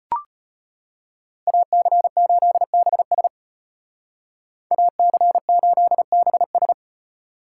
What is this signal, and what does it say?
Morse code, heard as a keyed tone, about 33 wpm, 700 Hz. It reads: AC8BS AC8BS